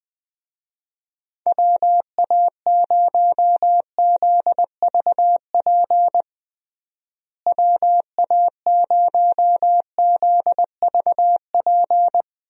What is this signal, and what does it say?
Morse code, heard as a keyed tone, about 20 wpm, 700 Hz. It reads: WA0ZVP WA0ZVP
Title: USA Call Signs - 6 Characters - Repeated 2x - 20wpm